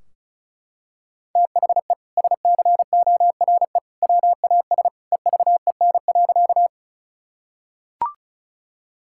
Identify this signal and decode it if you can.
Morse code, heard as a keyed tone, about 34 wpm, 700 Hz. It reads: THE SCORE WAS EVEN.
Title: Sentences from Top 1600 Words 35wpm